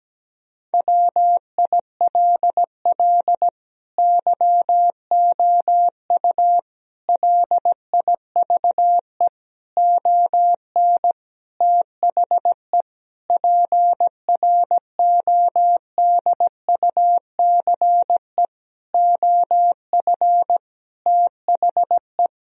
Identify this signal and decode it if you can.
Morse code, heard as a keyed tone, about 17 wpm, 700 Hz. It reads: WILL YOU LIVE ON THE PRODUCE OF THE
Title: Sentences from Top 1000 Words - Review Part 3 - 17wpm